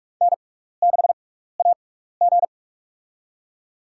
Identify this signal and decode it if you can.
Morse code, heard as a keyed tone, about 46 wpm, spaced as for 16 wpm, 700 Hz. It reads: N6AG
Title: USA Call Signs - 6x Character Spacing - 45wpm